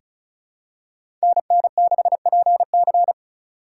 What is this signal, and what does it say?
Morse code, heard as a keyed tone, about 36 wpm, 700 Hz. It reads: NN6PC
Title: USA Call Signs - 5 Characters - 3x Character Spacing - 35wpm